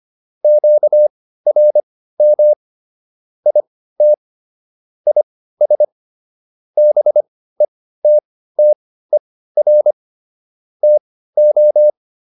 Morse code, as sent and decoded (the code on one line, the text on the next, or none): --.- .-. -- / .. - / .. ... / -... - - . .-. / - ---
QRM IT IS BETTER TO